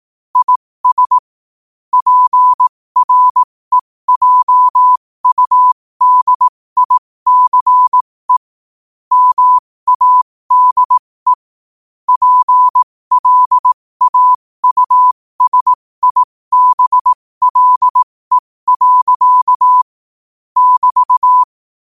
Morse code, heard as a keyed tone, about 18 wpm, spaced as for 15 wpm, 1000 Hz. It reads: IS PREJUDICE MADE PLAUSIBLE. <BT>
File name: qotdfiles / 15wpm